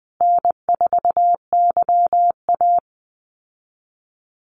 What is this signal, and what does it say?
Morse code, heard as a keyed tone, about 20 wpm, 700 Hz. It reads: N4YA
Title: Call Signs Repeated 3x - 20wpm